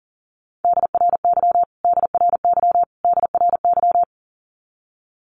Morse code, heard as a keyed tone, about 37 wpm, 700 Hz. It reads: DRY DRY DRY
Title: Top 500 Words Repeated 40wpm